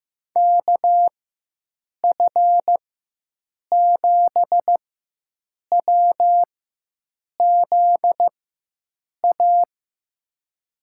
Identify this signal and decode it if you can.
Morse code, heard as a keyed tone, about 16 wpm, spaced as for 7 wpm, 700 Hz. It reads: KF7WZA
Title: USA Call Signs - 6 Characters - 4x Character Spacing - 15wpm